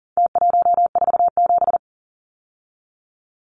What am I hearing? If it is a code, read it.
Morse code, heard as a keyed tone, about 40 wpm, 700 Hz. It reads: T147
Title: Numbers - Contest Style - 4-Digit 40wpm